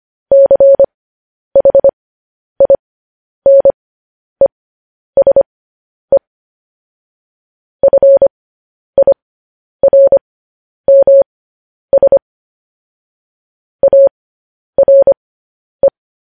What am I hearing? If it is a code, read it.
Morse code, heard as a keyed tone, about 25 wpm, spaced as for 10 wpm, 550 Hz. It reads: CHINESE FIRMS ARE
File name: podcast / News headlines-1x-25x10